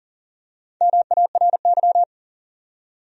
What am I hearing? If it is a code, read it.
Morse code, heard as a keyed tone, about 40 wpm, 700 Hz. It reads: MARY
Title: USA Common Names - 40wpm